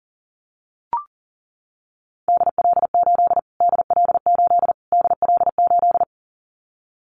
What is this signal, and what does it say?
Morse code, heard as a keyed tone, about 40 wpm, 700 Hz. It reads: DL8 DL8 DL8
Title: Callsign Prefixes for Top 25 DXCC - Repeated 3x - 40wpm